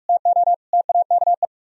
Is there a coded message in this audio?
Yes: Morse code from 0.1 to 1.5 s, heard as a keyed tone, about 44 wpm, 700 Hz. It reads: TO TAKE